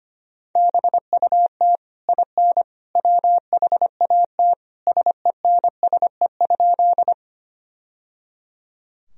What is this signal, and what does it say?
Morse code, heard as a keyed tone, about 25 wpm, 700 Hz. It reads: BUT IN WHAT SENSE?
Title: Sentences from Top 1000 Words - Review Part 4 - 25wpm